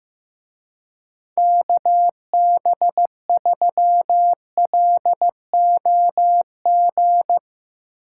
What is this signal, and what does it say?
Morse code, heard as a keyed tone, about 16 wpm, 700 Hz. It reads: KB3LOG
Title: Speed Racing - USA 2x3 Call Signs 15wpm